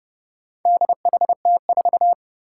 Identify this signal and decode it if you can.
Morse code, heard as a keyed tone, about 30 wpm, 700 Hz. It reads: DHT4
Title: Four Challenging Letter-Number Combinations 30wpm